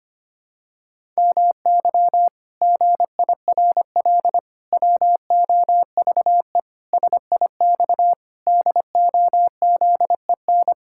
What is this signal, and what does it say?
Morse code, heard as a keyed tone, about 25 wpm, 700 Hz. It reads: MY GIRL WOVE SIX DOZEN
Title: Pangram Challenge v2 - 25wpm